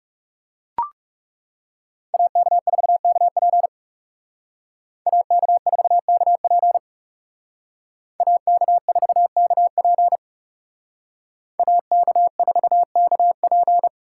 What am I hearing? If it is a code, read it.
Morse code, heard as a keyed tone, about 36 wpm, 700 Hz. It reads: AK4KP AK4KP AK4KP AK4KP
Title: Speed Racing - USA 2x2 Call Signs 30wpm